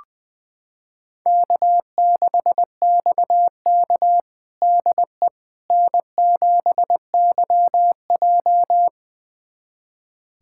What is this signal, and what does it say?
Morse code, heard as a keyed tone, about 20 wpm, 700 Hz. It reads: K6XK DE N7YJ
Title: USA Callsign DE Callsign - 4 Characters - 20wpm